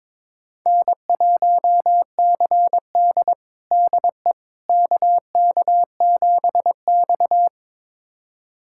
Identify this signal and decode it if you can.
Morse code, heard as a keyed tone, about 22 wpm, 700 Hz. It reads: N1CD DE KK7X
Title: USA Callsign DE Callsign - 4 Characters - 22wpm